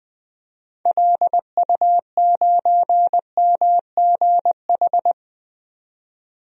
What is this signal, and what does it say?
Morse code, heard as a keyed tone, about 20 wpm, 700 Hz. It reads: LU9MGH
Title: DXCC-100 Call Signs - 6 Characters - 20wpm